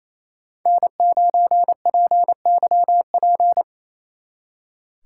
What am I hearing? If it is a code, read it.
Morse code, heard as a keyed tone, about 33 wpm, 700 Hz. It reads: N9PYP
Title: Speed Racing - USA 1x3 Call Signs 28wpm